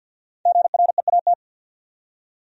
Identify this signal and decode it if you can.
Morse code, heard as a keyed tone, about 66 wpm, spaced as for 58 wpm, 700 Hz. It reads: GREAT
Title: Speed-Racing Top 200 Words 50wpm